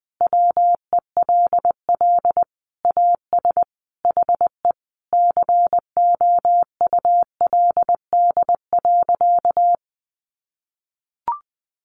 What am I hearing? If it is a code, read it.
Morse code, heard as a keyed tone, about 20 wpm, 700 Hz. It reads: WELL AS HE COULD.
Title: Sentences from Top 500 Words 20wpm